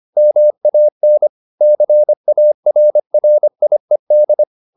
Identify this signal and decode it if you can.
Morse code, heard as a keyed tone, about 25 wpm, 600 Hz. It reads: MAN CARRIED
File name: the old man and the sea part 1 25wpm